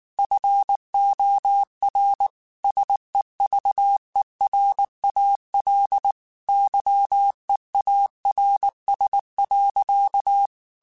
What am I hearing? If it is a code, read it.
Morse code, heard as a keyed tone, about 19 wpm, 800 Hz. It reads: FOR SEVERAL YEARS.
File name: QSOs / QSOs-mp3s / kc7haa@18